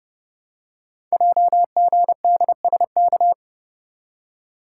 Morse code, heard as a keyed tone, about 30 wpm, 700 Hz. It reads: JGDSK
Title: Code Groups - 5 Letter:Numbers - 30wpm